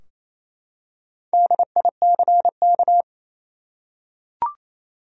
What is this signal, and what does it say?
Morse code, heard as a keyed tone, about 28 wpm, 700 Hz. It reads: DICK
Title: USA Common Names - 28wpm